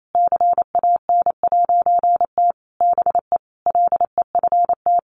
Morse code, heard as a keyed tone, about 28 wpm, 700 Hz. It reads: CAN'T BE LEFT
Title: A Sound Of Thunder 28wpm